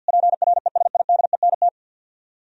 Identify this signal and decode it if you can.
Morse code, heard as a keyed tone, about 65 wpm, 700 Hz. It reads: PRESIDENT